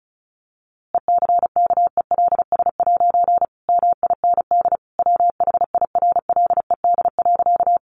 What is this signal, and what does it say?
Morse code, heard as a keyed tone, about 34 wpm, 700 Hz. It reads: ECKELS' MIND WHIRLED.
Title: A Sound Of Thunder 35wpm